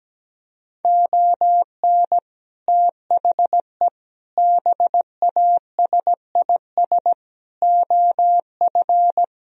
Morse code, heard as a keyed tone, about 17 wpm, 700 Hz. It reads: ON THE BASIS OF
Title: Speed-Racing - Sets of 5 Words 17wpm